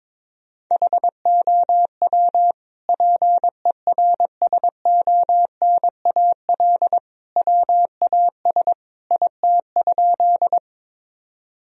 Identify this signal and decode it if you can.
Morse code, heard as a keyed tone, about 22 wpm, 700 Hz. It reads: HOW PERSONAL WAS IT?
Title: Sentences from Top 1000 Words - Review Part 4 - 22wpm